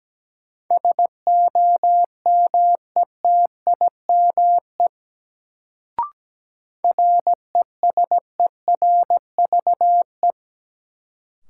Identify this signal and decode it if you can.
Morse code, heard as a keyed tone, about 17 wpm, 700 Hz. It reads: SOMETIME RESERVE
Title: Sentences from Top 1000 Words - Review Part 6 - New Word Drill - 17wpm